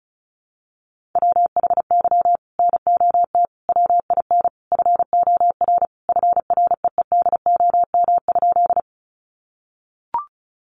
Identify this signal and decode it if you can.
Morse code, heard as a keyed tone, about 34 wpm, 700 Hz. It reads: WHY NOT WIN FOR FREEDOM?
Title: Sentences from Top 1500 Words 35wpm